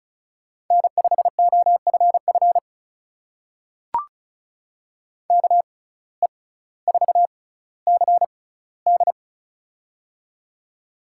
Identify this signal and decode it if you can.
Morse code, heard as a keyed tone, about 36 wpm, spaced as for 18 wpm, 700 Hz. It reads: N5OFF KE4CD
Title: USA Call Signs - 5 Characters - 6x Character Spacing - 35wpm